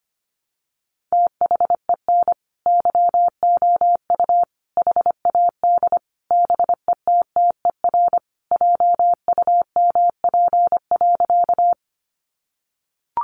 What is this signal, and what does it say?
Morse code, heard as a keyed tone, about 25 wpm, 700 Hz. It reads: THEN YOU HAD BETTER JUMP.
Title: Sentences from Top 1300 Words 25wpm